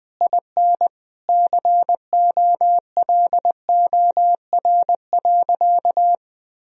Morse code, heard as a keyed tone, about 21 wpm, 700 Hz. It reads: IN COLOR.
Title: Sentences from Top 1000 Words - Review Part 2 - 20wpm